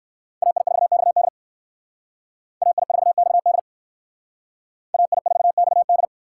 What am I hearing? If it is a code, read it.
Morse code, heard as a keyed tone, about 60 wpm, 700 Hz. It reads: AI4XD AI4XD AI4XD